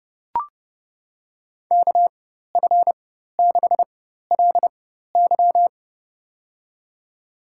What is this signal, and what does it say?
Morse code, heard as a keyed tone, about 30 wpm, spaced as for 14 wpm, 700 Hz. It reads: KF6LY
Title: USA Call Signs - 5 Characters - 4x Character Spacing - 30wpm